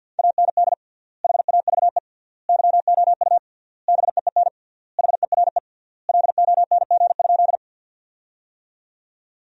Morse code, heard as a keyed tone, about 50 wpm, 700 Hz. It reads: AND HAVE YOU BEEN HERE LONG?
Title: 3x Word-Spacing - Sentences from Top 200 Words - 50wpm